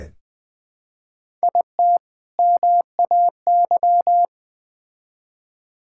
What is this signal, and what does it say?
Morse code, heard as a keyed tone, about 20 wpm, 700 Hz. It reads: IT MAY
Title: Build A Phrase - 3-Word Phrases - 20wpm